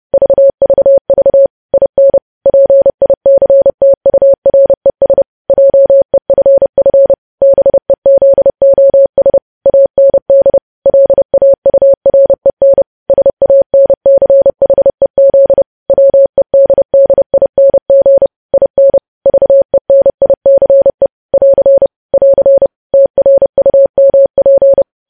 Morse code, heard as a keyed tone, about 30 wpm, 550 Hz. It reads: VVV IN PICTURES JEFF BEZOS AND LAUREN SANCHEZ WEDDING IN VENICE <AR> <AR> TRUMP
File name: podcast / News headlines-1x-30x30